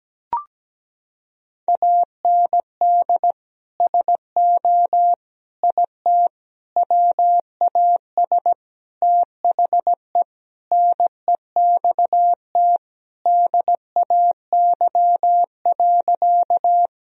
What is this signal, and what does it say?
Morse code, heard as a keyed tone, about 17 wpm, 700 Hz. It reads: AND SO IT WAS THE NEXT DAY.
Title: Sentences from Top 1000 Words - Review Part 1 - 17wpm